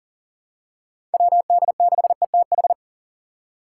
Morse code, heard as a keyed tone, about 40 wpm, 700 Hz. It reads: WD6ETH
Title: USA Call Signs - 6 Characters - 2x Character Spacing - 40wpm